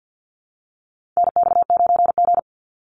Morse code, heard as a keyed tone, about 50 wpm, 700 Hz. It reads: NX9G
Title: Call Signs - 50wpm